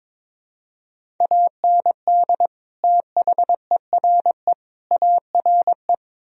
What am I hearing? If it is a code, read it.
Morse code, heard as a keyed tone, about 22 wpm, 700 Hz. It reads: AND THERE ARE
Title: Sentences from Top 1000 Words - Review Part 1 - 22wpm